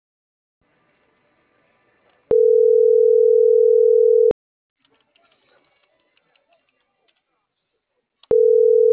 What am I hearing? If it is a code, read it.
Ringback tone (North America), 440 + 480 Hz.